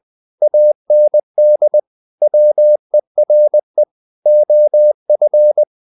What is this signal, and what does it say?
Morse code, heard as a keyed tone, about 20 wpm, 600 Hz. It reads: AND WERE OF